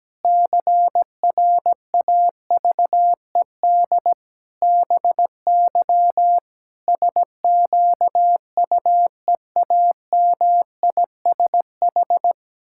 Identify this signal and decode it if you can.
Morse code, heard as a keyed tone, about 17 wpm, 700 Hz. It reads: CRAVED BY SQUEAMISH
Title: Pangram Challenge v2 - 17wpm